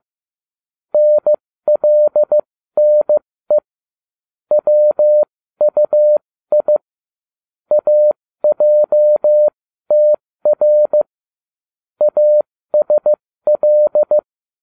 Morse code, heard as a keyed tone, about 15 wpm, spaced as for 13 wpm, 600 Hz. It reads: NLNE WUI AJTR ASL